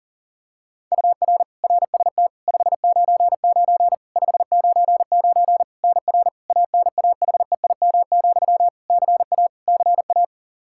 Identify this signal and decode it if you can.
Morse code, heard as a keyed tone, about 40 wpm, 700 Hz. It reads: UR RST 599 599 NR ANAHEIM, CA CA
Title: Practice QSO - Exchange 1 - Beginning - 40wpm